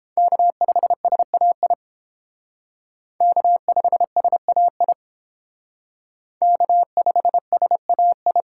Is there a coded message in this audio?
Yes: Morse code from 0.2 to 8.4 s, heard as a keyed tone, about 30 wpm, 700 Hz. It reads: K5SAI K5SAI K5SAI